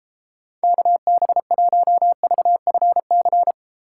Morse code, heard as a keyed tone, about 30 wpm, 700 Hz. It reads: KB1VFC